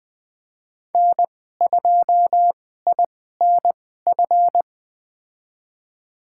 Morse code, heard as a keyed tone, about 20 wpm, spaced as for 14 wpm, 700 Hz. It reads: N2INF
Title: USA Call Signs - 2x Character Spacing - 20wpm